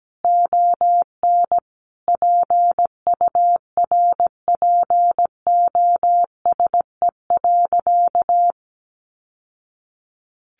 Morse code, heard as a keyed tone, about 17 wpm, 700 Hz. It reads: ON PURPOSE.
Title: Sentences from Top 800 Words 17wpm